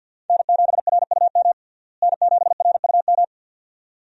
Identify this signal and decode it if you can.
Morse code, heard as a keyed tone, about 50 wpm, 700 Hz. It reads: N7RUM N7RUM
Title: USA Call Signs - 5 Characters - Repeated 2x - 50wpm